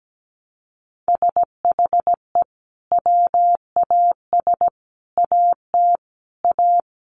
Morse code, heard as a keyed tone, about 17 wpm, 700 Hz. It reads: SHE WAS AT A
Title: Sentences from Top 1300 Words 17wpm